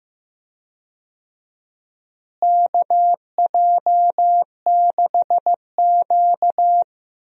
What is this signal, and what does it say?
Morse code, heard as a keyed tone, about 15 wpm, 700 Hz. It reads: KJ6Q
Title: USA Call Signs - 4 Characters - Repeated 3x - 15wpm